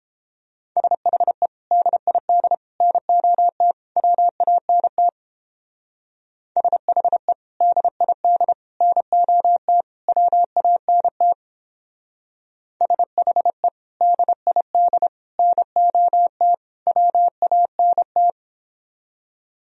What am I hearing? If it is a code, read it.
Morse code, heard as a keyed tone, about 29 wpm, 700 Hz. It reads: SHE DID NOT WANT SHE DID NOT WANT SHE DID NOT WANT